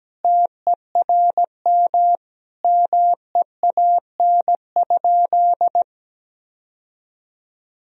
Morse code, heard as a keyed tone, about 17 wpm, 700 Hz. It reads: TERM MEAN?
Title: Sentences from Top 1000 Words - Review Part 4 - 17wpm